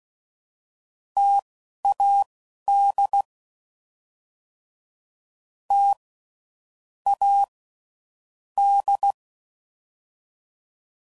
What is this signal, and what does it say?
Morse code, heard as a keyed tone, about 16 wpm, spaced as for 11 wpm, 800 Hz. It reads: TAD T A D
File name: Combinations / Difficult Combos / T AOEI's